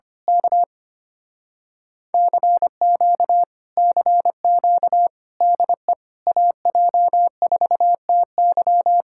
Morse code, heard as a keyed tone, about 24 wpm, 700 Hz. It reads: K CQ CQ DE AJ4TY
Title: Speed-Racing Common QSO Phrases - 25wpm